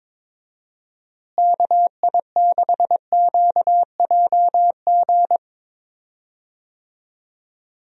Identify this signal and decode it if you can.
Morse code, heard as a keyed tone, about 22 wpm, 700 Hz. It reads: KI6QJG
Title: USA Call Signs - 6 Characters - Repeated 2x - 22wpm